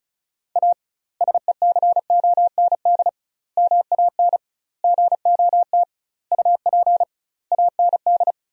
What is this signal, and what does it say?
Morse code, heard as a keyed tone, about 34 wpm, 700 Hz. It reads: A SECOND MAN GOT UP AND
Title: 2x Word-Spacing - Sentences from Top 300 Words - 35wpm